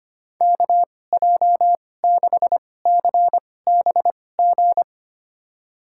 Morse code, heard as a keyed tone, about 25 wpm, spaced as for 18 wpm, 700 Hz. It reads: KJ6CBG
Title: USA Call Signs - 6 Characters - 2x Character Spacing - 25wpm